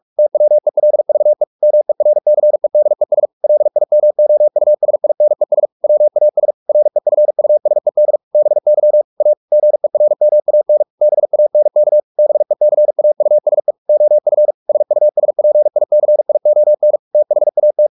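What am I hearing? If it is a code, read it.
Morse code, heard as a keyed tone, about 44 wpm, 600 Hz. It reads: TWELVE MERCEDES LIMOUSINES WAS REFUSED BY A GERMAN BANK BECAUSE OF SUSPICION THAT
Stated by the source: Morse practice at 45 wpm, the novel Never by Ken Follett in English, from CW med Gnister.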